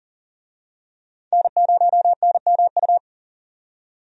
Morse code, heard as a keyed tone, about 38 wpm, 700 Hz. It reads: N0NMU
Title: USA Call Signs - 5 Characters - 40wpm